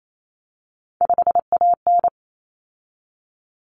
Morse code, heard as a keyed tone, about 28 wpm, 700 Hz. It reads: 5AN
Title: Common Cut Numbers - 3-Digit 28wpm